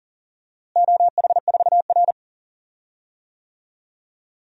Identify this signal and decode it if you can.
Morse code, heard as a keyed tone, about 40 wpm, 700 Hz. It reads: OH4R
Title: DXCC-100 Call Signs - 4 Characters - Repeated 3x - 40wpm